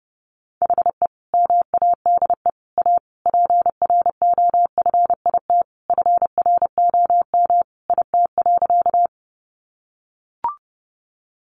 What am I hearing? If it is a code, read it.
Morse code, heard as a keyed tone, about 28 wpm, 700 Hz. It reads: HE MADE A PROFIT FROM IT.
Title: Sentences from Top 800 Words 30wpm